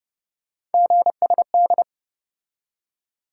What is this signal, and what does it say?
Morse code, heard as a keyed tone, about 30 wpm, 700 Hz. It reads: GSD
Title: USA Call Sign Postfixes - 3 Letters 30wpm